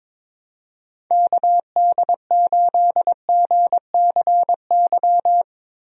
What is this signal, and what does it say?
Morse code, heard as a keyed tone, about 22 wpm, 700 Hz. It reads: KD8GCY
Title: USA Call Signs - 6 Characters - Repeated 3x - 22wpm